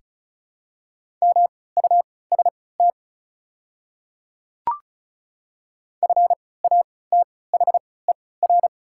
Morse code, heard as a keyed tone, about 36 wpm, spaced as for 20 wpm, 700 Hz. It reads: MUST FATHER
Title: 3x Character Spacing - Top 300 Words - 35wpm